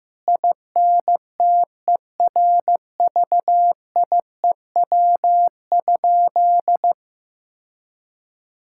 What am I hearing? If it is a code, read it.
Morse code, heard as a keyed tone, about 15 wpm, 700 Hz. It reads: INTERVIEW?
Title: Sentences from Top 1600 Words 15wpm